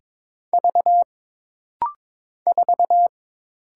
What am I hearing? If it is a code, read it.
Morse code, heard as a keyed tone, about 22 wpm, 700 Hz. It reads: V 4
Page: This rapid-fire timing is geared to push your ICR (Instant Character Recognition) skills.